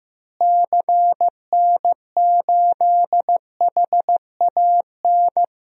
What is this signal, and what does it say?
Morse code, heard as a keyed tone, about 15 wpm, 700 Hz. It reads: CN8HAN
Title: DXCC-100 Call Signs - 6 Characters - Repeated 2x - 15wpm